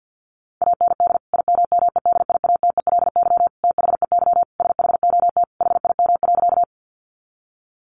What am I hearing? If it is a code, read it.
Morse code, heard as a keyed tone, about 50 wpm, 700 Hz. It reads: AND IMMEDIATELY THEY SHOT HIM.
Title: Sentences from Top 1100 Words 50wpm